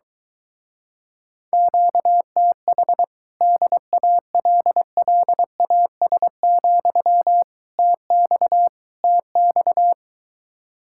Morse code, heard as a keyed tone, about 22 wpm, 700 Hz. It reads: QTH DALLAS, TX TX